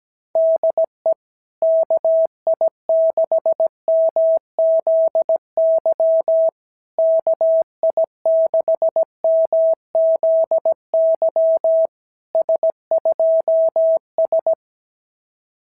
Morse code, heard as a keyed tone, about 15 wpm, 650 Hz. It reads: DE KI6MZY KI6MZY S2S
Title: SOTA QSO Practice 17wpm